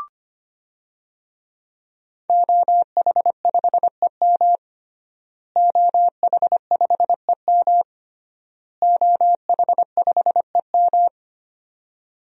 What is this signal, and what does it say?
Morse code, heard as a keyed tone, about 25 wpm, 700 Hz. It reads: OH5EM OH5EM OH5EM
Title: DXCC-100 Call Signs - 5 Characters - Repeated 3x - 25wpm